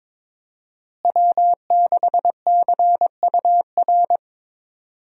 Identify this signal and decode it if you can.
Morse code, heard as a keyed tone, about 22 wpm, 700 Hz. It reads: W6CUR
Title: USA Call Signs - 5 Characters - Repeated 2x - 22wpm